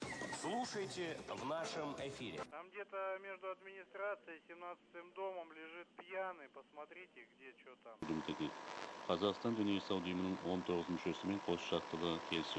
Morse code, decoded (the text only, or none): none